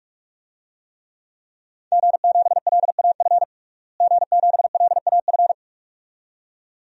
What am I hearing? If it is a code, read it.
Morse code, heard as a keyed tone, about 41 wpm, 700 Hz. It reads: G7LAF G7LAF
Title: DXCC-100 Call Signs - 5 Characters - Repeated 2x - 45wpm